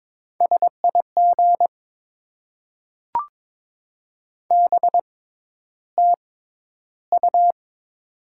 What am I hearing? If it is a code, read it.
Morse code, heard as a keyed tone, about 22 wpm, spaced as for 11 wpm, 700 Hz. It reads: SIG BTU
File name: Top 90 QSO Elements - 6x Character Spacing - 22wpm